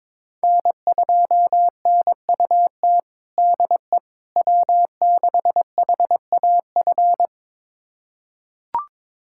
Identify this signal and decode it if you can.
Morse code, heard as a keyed tone, about 22 wpm, 700 Hz. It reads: N2NUT DE W6HAF
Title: USA Callsign DE Callsign - 5 Characters - 22wpm